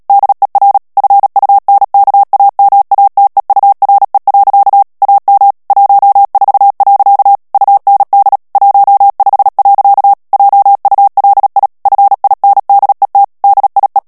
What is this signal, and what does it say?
Morse code, heard as a keyed tone, about 37 wpm, 800 Hz. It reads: DER FUNKAMATEURE. AM 14. UND 15. JULI FINDET DIE
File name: morse mp3 / ALLEMAND texte clair 2 vitesse 35 mots minute